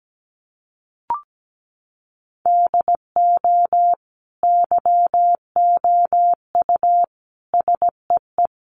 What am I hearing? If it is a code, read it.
Morse code, heard as a keyed tone, about 17 wpm, 700 Hz. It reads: DO YOU SEE
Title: Sentences from Top 1200 Words 17wpm